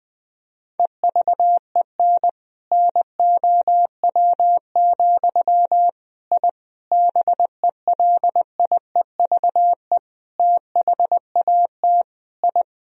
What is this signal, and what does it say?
Morse code, heard as a keyed tone, about 20 wpm, 700 Hz. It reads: EVEN NOW, I BELIEVE THAT I
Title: Sentences from Top 1000 Words - Review Part 1 - 20wpm